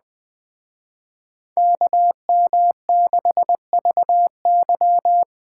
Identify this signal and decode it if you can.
Morse code, heard as a keyed tone, about 18 wpm, 700 Hz. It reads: KM6VY
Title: Speed Racing - USA Call Signs 17wpm